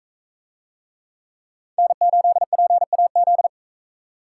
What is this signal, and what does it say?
Morse code, heard as a keyed tone, about 40 wpm, 700 Hz. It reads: N8PAZ